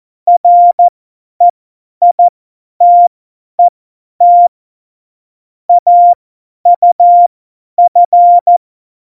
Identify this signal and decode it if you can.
Morse code, heard as a keyed tone, about 14 wpm, spaced as for 10 wpm, 700 Hz. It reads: REITET AUF
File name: Morsekurs / Witze / Witz 04 August 2020, 50 F70